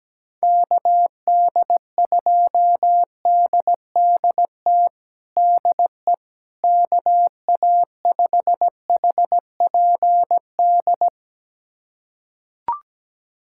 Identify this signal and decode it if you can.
Morse code, heard as a keyed tone, about 17 wpm, 700 Hz. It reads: KD2DDT DE KA5HPD
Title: USA Callsign DE Callsign - 6 Characters - 17wpm